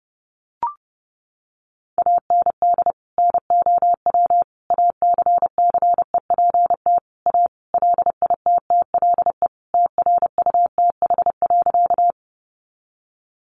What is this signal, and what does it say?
Morse code, heard as a keyed tone, about 30 wpm, 700 Hz. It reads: AND NOW ACCEPT A LITTLE TRUTH.
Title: Sentences from Top 1300 Words 30wpm